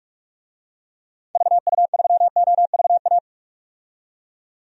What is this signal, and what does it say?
Morse code, heard as a keyed tone, about 42 wpm, 700 Hz. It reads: VU3OVA
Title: DXCC-100 Call Signs - 6 Characters - 45wpm